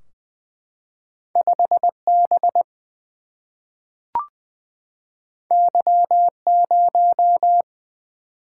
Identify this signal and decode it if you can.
Morse code, heard as a keyed tone, about 20 wpm, 700 Hz. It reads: 5 B Y 0
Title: Single-Letter Single-Number 20wpm